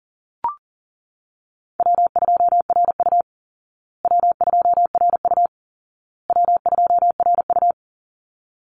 Morse code, heard as a keyed tone, about 40 wpm, 700 Hz. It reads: W2RU W2RU W2RU
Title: Easy Call Signs Repeated 3x - 40wpm